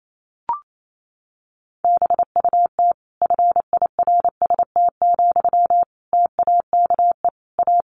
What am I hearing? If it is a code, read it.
Morse code, heard as a keyed tone, about 28 wpm, 700 Hz. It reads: BUT FIRST, TAKE A
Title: Sentences from Top 1400 Words 28wpm